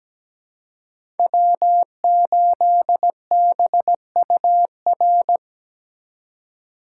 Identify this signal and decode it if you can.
Morse code, heard as a keyed tone, about 17 wpm, 700 Hz. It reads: W8BUR
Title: USA Call Signs - 2x Character Spacing - 17wpm